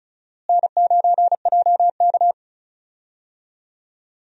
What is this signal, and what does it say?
Morse code, heard as a keyed tone, about 34 wpm, 700 Hz. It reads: N9JK
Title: USA Call Signs - 4 Characters - Repeated 2x - 35wpm